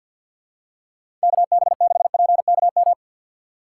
Morse code, heard as a keyed tone, about 45 wpm, 700 Hz. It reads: KB6PKM
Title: USA Call Signs - 6 Characters - 3x Character Spacing - 50wpm